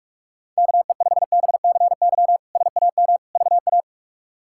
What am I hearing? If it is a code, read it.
Morse code, heard as a keyed tone, about 42 wpm, 700 Hz. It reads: KE5BCY SAM VA
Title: K1USN SST QSO Practice - 45wpm